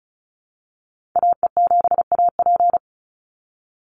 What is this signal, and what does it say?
Morse code, heard as a keyed tone, about 34 wpm, 700 Hz. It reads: AE7AP
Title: NA SOTA Activator Callsigns 35wpm